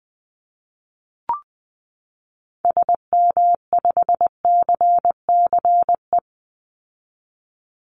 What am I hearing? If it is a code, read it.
Morse code, heard as a keyed tone, about 19 wpm, 700 Hz. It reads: SM5CCE
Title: Call Signs Repeated 3x - 20wpm